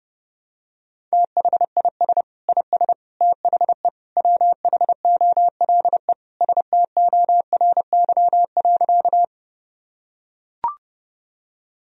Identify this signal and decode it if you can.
Morse code, heard as a keyed tone, about 30 wpm, 700 Hz. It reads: THIS IS THE WHOLE STORY.
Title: Sentences from Top 1000 Words - Review Part 1 - 30wpm